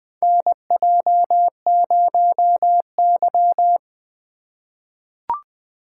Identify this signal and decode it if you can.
Morse code, heard as a keyed tone, about 20 wpm, 700 Hz. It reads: NJ0Y
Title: USA Call Signs - 4 Characters - 2x Character Spacing - 20wpm